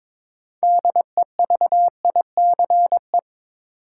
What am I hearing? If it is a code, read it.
Morse code, heard as a keyed tone, about 22 wpm, 700 Hz. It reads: DEVICE
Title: Sentences from Top 1500 Words - New Word Drill - 22wpm